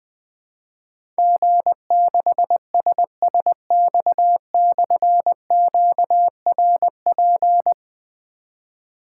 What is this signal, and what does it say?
Morse code, heard as a keyed tone, about 20 wpm, 700 Hz. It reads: G6SSX/QRP
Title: Difficult Call Signs 20wpm